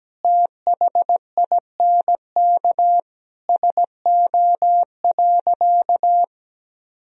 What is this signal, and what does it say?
Morse code, heard as a keyed tone, about 17 wpm, 700 Hz. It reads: THINK SO.
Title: Sentences from Top 1000 Words - Review Part 1 - 17wpm